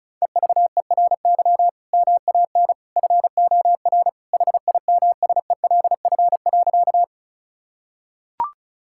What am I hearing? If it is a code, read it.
Morse code, heard as a keyed tone, about 36 wpm, 700 Hz. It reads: EVERY MAN FOR HIMSELF.
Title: Sentences from Top 1600 Words 35wpm